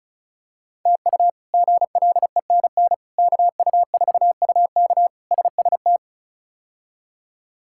Morse code, heard as a keyed tone, about 34 wpm, 700 Hz. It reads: TU GLENN KU4UK SST
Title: K1USN SST QSO Practice - 35wpm